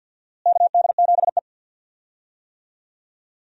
Morse code, heard as a keyed tone, about 53 wpm, 700 Hz. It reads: KD7E
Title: USA Call Signs - 4 Characters - Repeated 2x - 50wpm